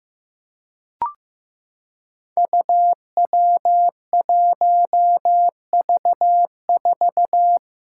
Morse code, heard as a keyed tone, about 15 wpm, 700 Hz. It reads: UW1V4
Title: Five Challenging Letter-Number Combinations 15wpm